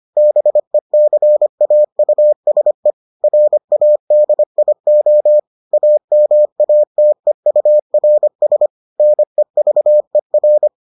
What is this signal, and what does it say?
Morse code, heard as a keyed tone, about 25 wpm, 600 Hz. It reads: BECAUSE RADIO AMATEURS NEVER